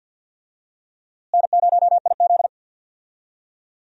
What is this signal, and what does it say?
Morse code, heard as a keyed tone, about 53 wpm, 700 Hz. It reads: N0IZ
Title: USA Call Signs - 4 Characters - Repeated 3x - 50wpm